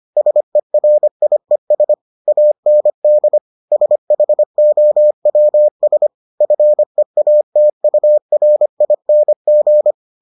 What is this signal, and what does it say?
Morse code, heard as a keyed tone, about 25 wpm, 600 Hz. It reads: SERIES AND SHOWS FEATURING